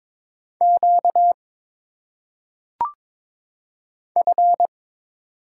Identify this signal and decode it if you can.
Morse code, heard as a keyed tone, about 22 wpm, 700 Hz. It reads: Q F
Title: Single Letters - Phonetic Pronunciation 22wpm